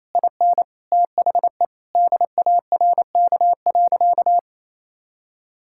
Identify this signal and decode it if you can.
Morse code, heard as a keyed tone, about 28 wpm, 700 Hz. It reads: IN THE DARK.
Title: Sentences from Top 1600 Words 28wpm